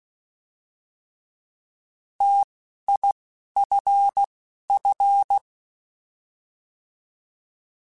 Morse code, heard as a keyed tone, about 16 wpm, spaced as for 11 wpm, 800 Hz. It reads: TIFF